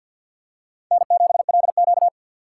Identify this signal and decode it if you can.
Morse code, heard as a keyed tone, about 50 wpm, 700 Hz. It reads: N7LQ
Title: USA Call Signs - 4 Characters - 6x Character Spacing - 50wpm